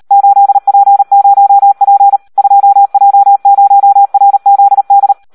Morse code, heard as a keyed tone, about 38 wpm, 800 Hz. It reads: 8P0P 2J0RZD
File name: Audio practica CW / SD 1 38